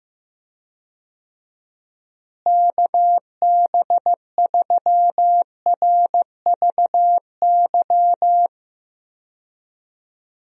Morse code, heard as a keyed tone, about 15 wpm, 700 Hz. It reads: KB3RVY
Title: USA Call Signs - 6 Characters - Repeated 3x - 15wpm